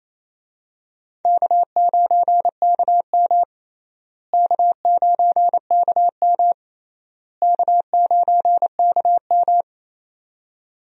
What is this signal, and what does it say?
Morse code, heard as a keyed tone, about 28 wpm, 700 Hz. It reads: K9KM K9KM K9KM
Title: USA Call Signs - 4 Characters - Repeated 3x - 28wpm